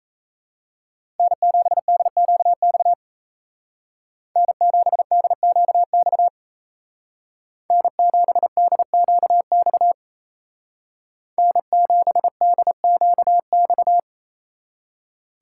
Morse code, heard as a keyed tone, about 33 wpm, 700 Hz. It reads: N7DQX N7DQX N7DQX N7DQX
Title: Speed Racing - USA 1x3 Call Signs 28wpm